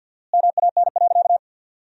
Morse code, heard as a keyed tone, about 50 wpm, 700 Hz. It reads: MAN.
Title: 4x Word-Spacing - Sentences from Top 400 Words - 50wpm